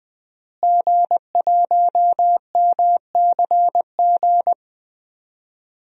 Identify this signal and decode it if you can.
Morse code, heard as a keyed tone, about 21 wpm, 700 Hz. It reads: G1MCG
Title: DXCC-100 Call Signs - 5 Characters - Repeated 3x - 20wpm